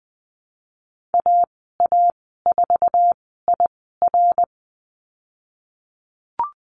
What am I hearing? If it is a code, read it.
Morse code, heard as a keyed tone, about 20 wpm, spaced as for 15 wpm, 700 Hz. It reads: AA4IR
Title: 076 Lesson 13g - Letter L - Call Signs 20wpm